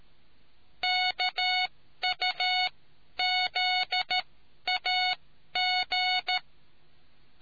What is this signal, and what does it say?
Morse code, heard as a keyed tone, about 13 wpm, spaced as for 11 wpm, 740 Hz. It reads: KUZAG